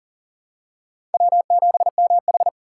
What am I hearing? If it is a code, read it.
Morse code, heard as a keyed tone, about 40 wpm, 700 Hz. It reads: W7MH
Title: USA Call Signs - 4 Characters - 3x Character Spacing - 40wpm